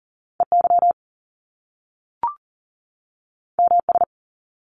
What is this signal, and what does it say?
Morse code, heard as a keyed tone, about 39 wpm, 700 Hz. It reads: EY MS
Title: Top Two Letter Combinations in Words 40wpm